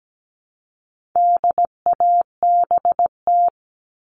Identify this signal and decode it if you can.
Morse code, heard as a keyed tone, about 17 wpm, 700 Hz. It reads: DABT
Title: All Cut Numbers - 4-Digit 17wpm